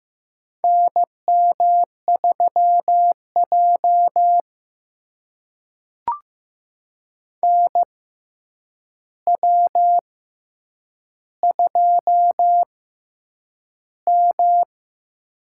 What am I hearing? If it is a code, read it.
Morse code, heard as a keyed tone, about 15 wpm, spaced as for 8 wpm, 700 Hz. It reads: NM3J NW2M
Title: USA Call Signs - 4 Characters - 6x Character Spacing - 15wpm